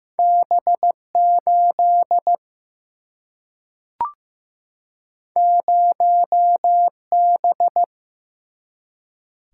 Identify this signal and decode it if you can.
Morse code, heard as a keyed tone, about 15 wpm, 700 Hz. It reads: B8 0B
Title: Single-Letter Single-Number 15wpm